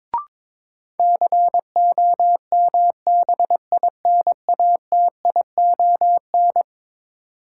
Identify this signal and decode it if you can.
Morse code, heard as a keyed tone, about 22 wpm, 700 Hz. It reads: COMBINATION
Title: Sentences from Top 1300 Words - New Word Drill Alt - 22wpm